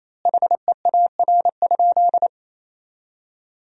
Morse code, heard as a keyed tone, about 28 wpm, 700 Hz. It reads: HEAR?
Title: 2x Word-Spacing - Sentences from Top 300 Words - 28wpm